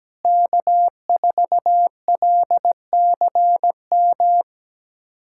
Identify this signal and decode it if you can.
Morse code, heard as a keyed tone, about 17 wpm, 700 Hz. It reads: K4LCM
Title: USA Callsign DE Callsign - 5 Characters - 17wpm